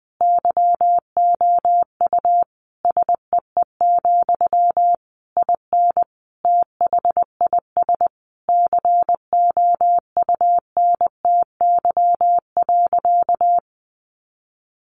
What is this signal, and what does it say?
Morse code, heard as a keyed tone, about 20 wpm, 700 Hz. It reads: YOU SEE, IN THIS COUNTY.
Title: Sentences from Top 1200 Words 20wpm